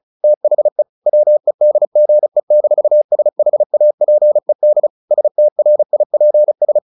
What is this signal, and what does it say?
Morse code, heard as a keyed tone, about 34 wpm, 600 Hz. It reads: THE WEDGE-SHAPED STRIPS
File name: the old man and the sea / the old man and the sea 1 35wpm